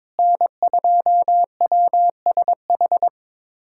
Morse code, heard as a keyed tone, about 22 wpm, 700 Hz. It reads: N2WSH
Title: USA Callsign DE Callsign - 5 Characters - 22wpm